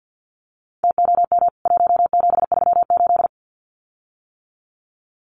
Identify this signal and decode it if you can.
Morse code, heard as a keyed tone, about 50 wpm, 700 Hz. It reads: TOM 1738
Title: CWT Exchange 50wpm